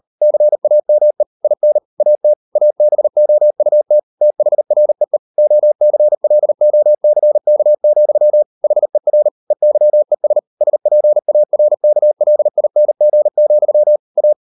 Morse code, heard as a keyed tone, about 38 wpm, 600 Hz. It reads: CAME IN AT ABOUT THREE OCLOCK, HER EYES SPARKLING, A